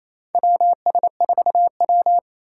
Morse code, heard as a keyed tone, about 28 wpm, 700 Hz. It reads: WS4W
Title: USA Call Signs - 2x Character Spacing - 28wpm